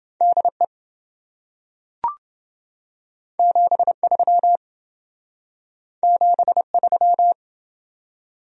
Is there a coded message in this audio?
Yes: Morse code from 0.2 to 7.3 s, heard as a keyed tone, about 27 wpm, 700 Hz. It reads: DE 73 73